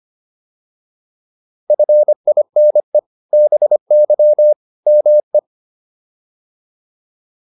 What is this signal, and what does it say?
Morse code, heard as a keyed tone, about 25 wpm, 600 Hz. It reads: FINE BY ME